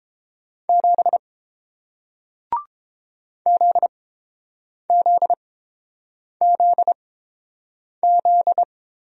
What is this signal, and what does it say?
Morse code, heard as a keyed tone, about 29 wpm, spaced as for 7 wpm, 690 Hz. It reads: 7 ZZZZ